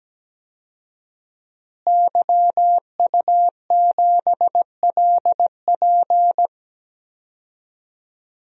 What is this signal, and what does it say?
Morse code, heard as a keyed tone, about 17 wpm, 700 Hz. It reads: YU7LP
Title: DXCC-100 Call Signs - 5 Characters - Repeated 2x - 17wpm